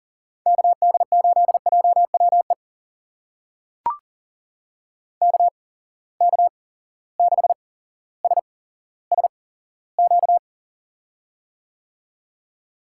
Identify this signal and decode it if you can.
Morse code, heard as a keyed tone, about 39 wpm, spaced as for 17 wpm, 700 Hz. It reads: KD8JWE KK6SSQ
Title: USA Call Signs - 6 Characters - 8x Character Spacing - 40wpm